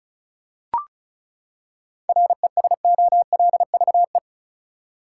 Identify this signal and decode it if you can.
Morse code, heard as a keyed tone, about 36 wpm, 700 Hz. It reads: RESOLVE
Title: Seven Letter Words - 35wpm